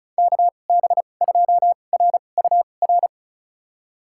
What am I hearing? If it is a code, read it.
Morse code, heard as a keyed tone, about 36 wpm, spaced as for 25 wpm, 700 Hz. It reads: KB2RUR
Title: USA Call Signs - 6 Characters - 2x Character Spacing - 35wpm